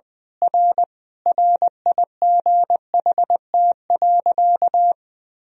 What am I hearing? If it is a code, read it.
Morse code, heard as a keyed tone, about 20 wpm, 700 Hz. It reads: R RIGHT.
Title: Sentences from Top 500 Words and Abbreviations 20wpm